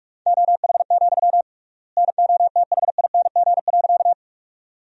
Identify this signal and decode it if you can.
Morse code, heard as a keyed tone, about 46 wpm, 700 Hz. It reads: OH, NOTHING.
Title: 3x Word-Spacing - Sentences from Top 500 Words - 45wpm